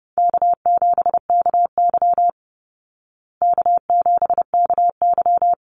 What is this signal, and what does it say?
Morse code, heard as a keyed tone, about 28 wpm, 700 Hz. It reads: K7KY K7KY
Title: Easy Call Signs Repeated 2x - 30wpm